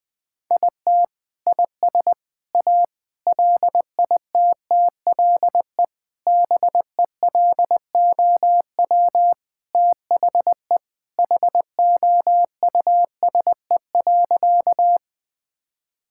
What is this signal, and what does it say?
Morse code, heard as a keyed tone, about 20 wpm, 700 Hz. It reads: IT IS A LITTLE BELOW THE HOUSE.
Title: Sentences from Top 1000 Words - Review Part 5 - 20wpm